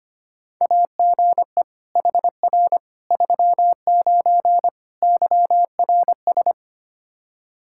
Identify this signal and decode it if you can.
Morse code, heard as a keyed tone, about 25 wpm, 700 Hz. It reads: AGE HR 39 YRS
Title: Practice QSO - Exchange 3 - More Info - 25wpm